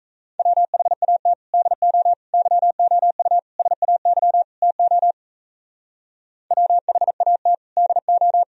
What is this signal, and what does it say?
Morse code, heard as a keyed tone, about 35 wpm, 700 Hz. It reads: WHAT DO YOU SAY TO WHAT DO
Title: Speed-Racing - Sets of 5 Words 28wpm